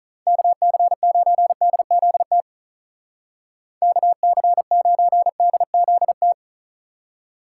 Morse code, heard as a keyed tone, about 45 wpm, 700 Hz. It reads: KC9DZT KC9DZT